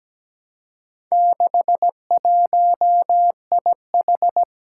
Morse code, heard as a keyed tone, about 17 wpm, 700 Hz. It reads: 61IH
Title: Four Challenging Letter-Number Combinations 17wpm